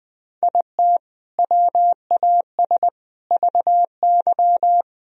Morse code, heard as a keyed tone, about 20 wpm, 700 Hz. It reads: IT WAS VY
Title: Sentences from Top 500 Words and Abbreviations 20wpm